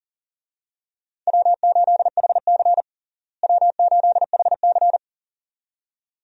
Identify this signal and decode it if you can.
Morse code, heard as a keyed tone, about 37 wpm, 700 Hz. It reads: W8HC W8HC
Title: USA Call Signs - 4 Characters - Repeated 2x - 40wpm